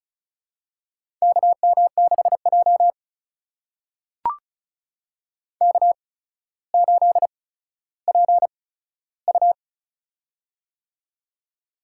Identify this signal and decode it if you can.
Morse code, heard as a keyed tone, about 36 wpm, spaced as for 15 wpm, 700 Hz. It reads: KM6J K8PU
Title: USA Call Signs - 4 Characters - 8x Character Spacing - 35wpm